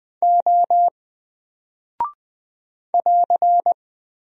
Morse code, heard as a keyed tone, about 20 wpm, 700 Hz. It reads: O <AR>
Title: Single Character (Letter:Number:Punctuation:Prosign) 20wpm